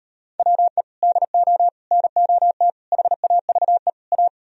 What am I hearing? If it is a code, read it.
Morse code, heard as a keyed tone, about 38 wpm, 700 Hz. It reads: WE DO NOT HAVE A